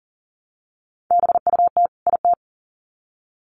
Morse code, heard as a keyed tone, about 40 wpm, 700 Hz. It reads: BUT IT
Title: Sets of 2 Words 40wpm